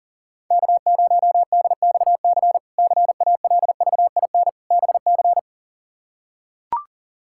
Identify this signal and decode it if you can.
Morse code, heard as a keyed tone, about 40 wpm, 700 Hz. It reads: K0DXC CALVIN BC
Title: K1USN SST QSO Practice - 40wpm